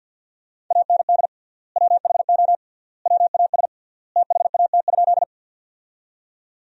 Morse code, heard as a keyed tone, about 52 wpm, 700 Hz. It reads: AND WHO WAS THAT?
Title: 3x Word-Spacing - Sentences from Top 100 Words - 50wpm